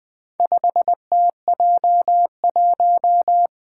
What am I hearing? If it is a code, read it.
Morse code, heard as a keyed tone, about 20 wpm, 700 Hz. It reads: 5TJ1
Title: Four Challenging Letter-Number Combinations 20wpm